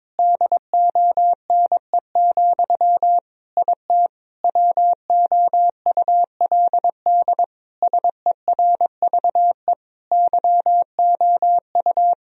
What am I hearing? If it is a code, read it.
Morse code, heard as a keyed tone, about 22 wpm, 700 Hz. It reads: DONE, IT WOULD SERVE YOU
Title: Sentences from Top 1000 Words - Review Part 2 - 22wpm